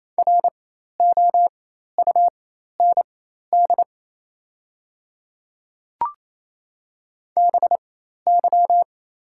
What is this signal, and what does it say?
Morse code, heard as a keyed tone, about 28 wpm, spaced as for 13 wpm, 700 Hz. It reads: ROUND BY
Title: 4x Character Spacing - Top 300 Words - 28wpm